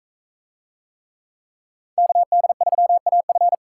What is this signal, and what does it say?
Morse code, heard as a keyed tone, about 42 wpm, 700 Hz. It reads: KD3AF